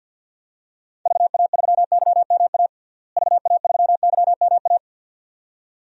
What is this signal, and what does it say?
Morse code, heard as a keyed tone, about 50 wpm, 700 Hz. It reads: VA3YMA VA3YMA
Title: DXCC-100 Call Signs - 6 Characters - Repeated 2x - 50wpm